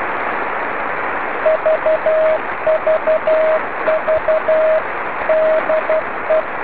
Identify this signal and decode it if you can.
Morse code, heard as a keyed tone, about 12 wpm, 630 Hz, with radio noise. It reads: VVV DE